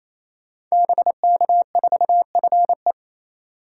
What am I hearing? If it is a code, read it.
Morse code, heard as a keyed tone, about 27 wpm, 700 Hz. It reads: BK4FE
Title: Code Groups - 5 Letter:Numbers - 28wpm